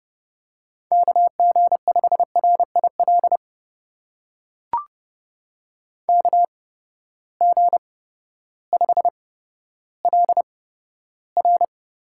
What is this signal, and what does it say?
Morse code, heard as a keyed tone, about 28 wpm, spaced as for 14 wpm, 700 Hz. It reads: KG5RIL KG5LR
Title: USA Call Signs - 6 Characters - 8x Character Spacing - 30wpm